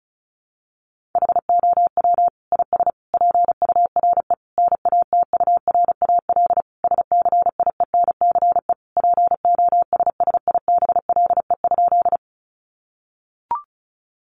Morse code, heard as a keyed tone, about 34 wpm, 700 Hz. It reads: HOW IS PURE NATURAL SCIENCE POSSIBLE?
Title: Sentences from Top 1500 Words 35wpm